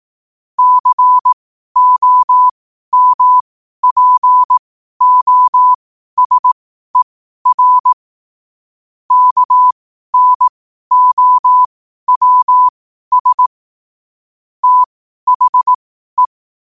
Morse code, heard as a keyed tone, about 18 wpm, spaced as for 12 wpm, 1000 Hz. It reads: COMPOSER KNOWS THE
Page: New quotes every day in morse code at 12 Words per minute.